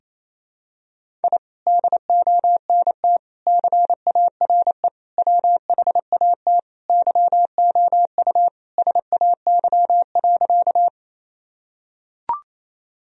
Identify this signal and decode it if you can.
Morse code, heard as a keyed tone, about 28 wpm, 700 Hz. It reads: I DONT CARE WHAT YOU SAY.
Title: Sentences from Top 1000 Words - Review Part 1 - 28wpm